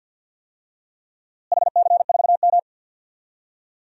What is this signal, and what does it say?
Morse code, heard as a keyed tone, about 46 wpm, 700 Hz. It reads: HC4M
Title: DXCC-100 Call Signs - 4 Characters - Repeated 3x - 50wpm